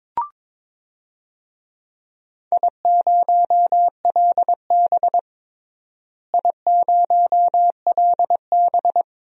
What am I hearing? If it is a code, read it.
Morse code, heard as a keyed tone, about 22 wpm, 700 Hz. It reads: I0LB I0LB
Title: DXCC-100 Call Signs - 4 Characters - Repeated 3x - 22wpm